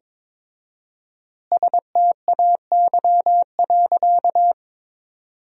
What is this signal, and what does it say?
Morse code, heard as a keyed tone, about 22 wpm, 700 Hz. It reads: STAY.
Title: 4x Word-Spacing - Sentences from Top 500 Words - 22wpm